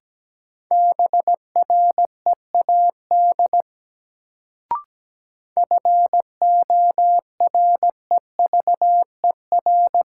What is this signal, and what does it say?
Morse code, heard as a keyed tone, about 17 wpm, 700 Hz. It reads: BREAD FOREVER
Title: Sentences from Top 1600 Words - New Word Drill - 17wpm